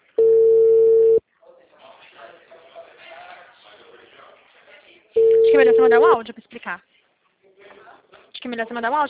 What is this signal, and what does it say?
Ringback tone (China, 450 Hz), with speech.
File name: out- -2018- - - .37624